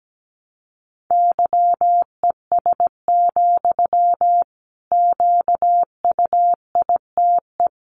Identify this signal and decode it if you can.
Morse code, heard as a keyed tone, about 17 wpm, 700 Hz. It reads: YES, QUITE
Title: Sentences from Top 700 Words 17wpm